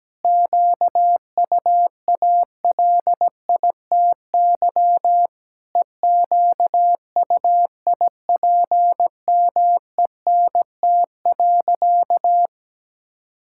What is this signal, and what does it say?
Morse code, heard as a keyed tone, about 17 wpm, 700 Hz. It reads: QUALITY EQUIPMENT.
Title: Sentences from Top 1000 Words - Review Part 6 - 17wpm